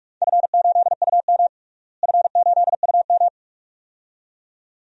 Morse code, heard as a keyed tone, about 41 wpm, 700 Hz. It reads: F8UM F8UM
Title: DXCC-100 Call Signs - 4 Characters - Repeated 2x - 45wpm